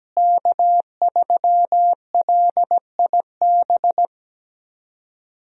Morse code, heard as a keyed tone, about 17 wpm, 700 Hz. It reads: K3LIB
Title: USA Call Signs - 5 Characters - Repeated 3x - 17wpm